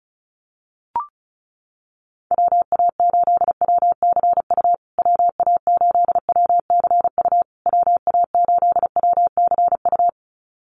Morse code, heard as a keyed tone, about 34 wpm, 700 Hz. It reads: WA8WCU WA8WCU WA8WCU
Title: USA Call Signs Repeated 3x 35wpm